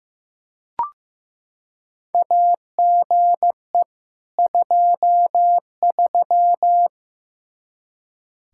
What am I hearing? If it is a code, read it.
Morse code, heard as a keyed tone, about 15 wpm, 700 Hz. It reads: AGE 23
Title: K1USN SST QSO Practice 15wpm